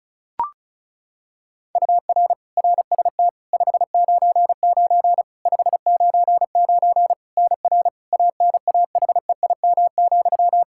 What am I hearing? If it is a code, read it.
Morse code, heard as a keyed tone, about 36 wpm, 700 Hz. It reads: UR RST 599 599 NR ANAHEIM,
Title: Practice QSO - Exchange 1 - Beginning - 35wpm